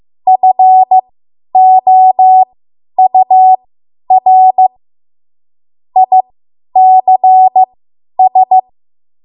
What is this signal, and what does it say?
Morse code, heard as a keyed tone, about 15 wpm, spaced as for 10 wpm, 750 Hz. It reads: FOUR ICS
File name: Morse / 10WPM